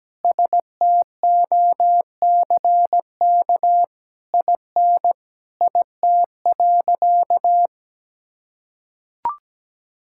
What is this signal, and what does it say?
Morse code, heard as a keyed tone, about 17 wpm, 700 Hz. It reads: STOCK IN IT.
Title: Sentences from Top 1000 Words - Review Part 5 - 17wpm